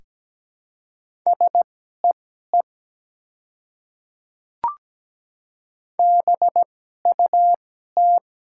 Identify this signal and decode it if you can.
Morse code, heard as a keyed tone, about 17 wpm, spaced as for 12 wpm, 700 Hz. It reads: SEE BUT
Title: 2x Character Spacing - Top 100 Words - 17wpm